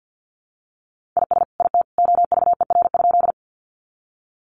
Morse code, heard as a keyed tone, about 50 wpm, 700 Hz. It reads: IS IT OVER?
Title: Sentences from Top 100 Words 50wpm